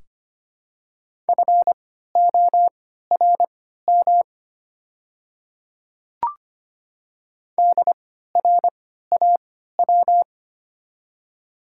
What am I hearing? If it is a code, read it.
Morse code, heard as a keyed tone, about 25 wpm, spaced as for 14 wpm, 700 Hz. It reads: FORM DRAW
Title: 3x Character Spacing - Top 500 Words - 25wpm